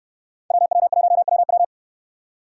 Morse code, heard as a keyed tone, about 69 wpm, 700 Hz. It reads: WW1WW